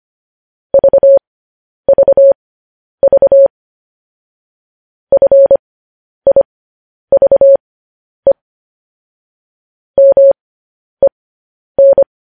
Morse code, heard as a keyed tone, about 25 wpm, spaced as for 10 wpm, 550 Hz. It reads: VVV FIVE MEN
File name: podcast / News headlines-1x-25x10